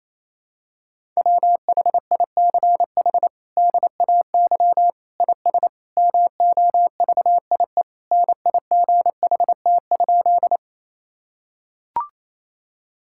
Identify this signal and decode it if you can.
Morse code, heard as a keyed tone, about 28 wpm, 700 Hz. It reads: WHICH DAY IS MOVIE NIGHT?
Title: Sentences from Top 1000 Words - Review Part 5 - 28wpm